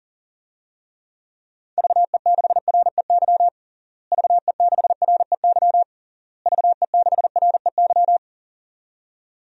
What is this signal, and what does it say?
Morse code, heard as a keyed tone, about 37 wpm, 700 Hz. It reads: VE6REY VE6REY VE6REY
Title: DXCC-100 Call Signs - 6 Characters - Repeated 3x - 40wpm